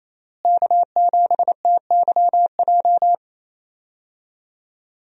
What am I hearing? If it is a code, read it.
Morse code, heard as a keyed tone, about 28 wpm, 700 Hz. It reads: K7TYJ
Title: USA Call Signs - 5 Characters - Repeated 2x - 28wpm